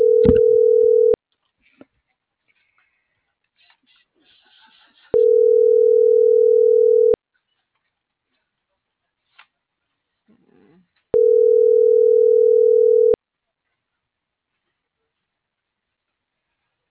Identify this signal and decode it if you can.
Ringback tone (North America), 440 + 480 Hz.